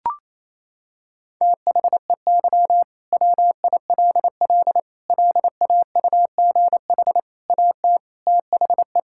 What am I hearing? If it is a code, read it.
Morse code, heard as a keyed tone, about 28 wpm, 700 Hz. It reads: THEY WILL LAUGH AT THE
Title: Sentences from Top 1000 Words - Review Part 6 - 28wpm